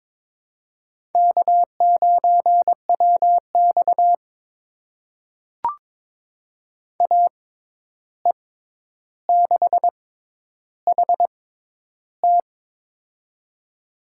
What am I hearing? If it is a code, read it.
Morse code, heard as a keyed tone, about 22 wpm, spaced as for 10 wpm, 700 Hz. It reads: K9WX AE6HT
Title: USA Call Signs - 6x Character Spacing - 22wpm